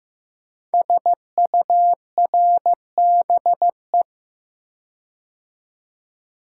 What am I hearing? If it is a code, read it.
Morse code, heard as a keyed tone, about 15 wpm, 700 Hz. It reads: SURBE
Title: Code Groups - 5 Letters - 15wpm